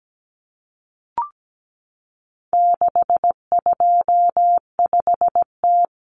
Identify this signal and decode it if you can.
Morse code, heard as a keyed tone, about 17 wpm, 700 Hz. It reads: 625T
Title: Common Cut Numbers - 4-Digit 17wpm